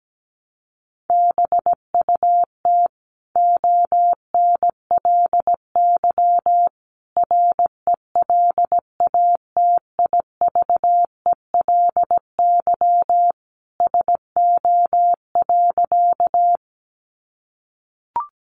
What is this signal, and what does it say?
Morse code, heard as a keyed tone, about 17 wpm, 700 Hz. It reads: BUT ONLY RELATIVELY SO.
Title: Sentences from Top 1100 Words 17wpm